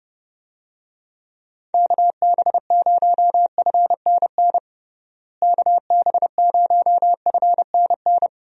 Morse code, heard as a keyed tone, about 32 wpm, 700 Hz. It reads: KB0FNN KB0FNN
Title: USA Call Signs - 6 Characters - Repeated 2x - 30wpm